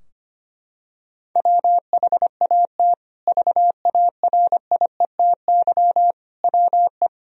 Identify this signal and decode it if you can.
Morse code, heard as a keyed tone, about 25 wpm, 700 Hz. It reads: WHAT VARIETY WE
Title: Sentences from Top 1000 Words - Review Part 4 - 25wpm